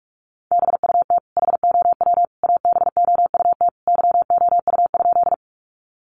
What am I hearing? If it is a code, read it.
Morse code, heard as a keyed tone, about 44 wpm, 700 Hz. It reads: BUT HOW ABOUT YOU?
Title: Sentences from Top 100 Words 45wpm